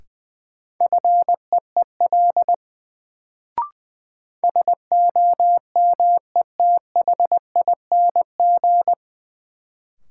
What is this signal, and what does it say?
Morse code, heard as a keyed tone, about 20 wpm, 700 Hz. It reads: FEEL SOMETHING
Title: Sets of 5 Words - New Word Drill - 20wpm